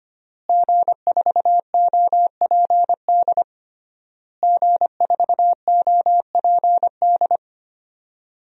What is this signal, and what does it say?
Morse code, heard as a keyed tone, about 25 wpm, 700 Hz. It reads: G4OPD G4OPD
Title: DXCC-100 Call Signs - 5 Characters - Repeated 3x - 25wpm